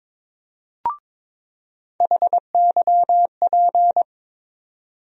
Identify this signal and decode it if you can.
Morse code, heard as a keyed tone, about 22 wpm, 700 Hz. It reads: HYP
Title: USA Call Sign Postfixes - 3 Letters 22wpm